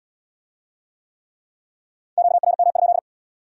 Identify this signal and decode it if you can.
Morse code, heard as a keyed tone, about 64 wpm, 700 Hz. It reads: QRM?